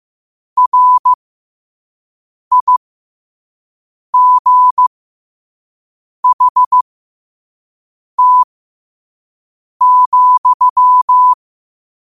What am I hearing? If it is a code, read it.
Morse code, heard as a keyed tone, about 15 wpm, spaced as for 5 wpm, 1000 Hz. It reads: RIGHT,
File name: qotdfiles / 5wpm